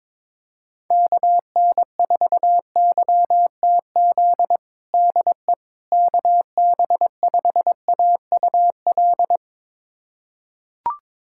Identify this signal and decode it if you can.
Morse code, heard as a keyed tone, about 22 wpm, 700 Hz. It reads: KN4YTZ DE KB5AUL
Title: USA Callsign DE Callsign - 6 Characters - 22wpm